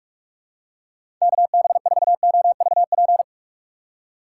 Morse code, heard as a keyed tone, about 42 wpm, 700 Hz. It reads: KB4OVP
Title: USA Call Signs - 6 Characters - 2x Character Spacing - 45wpm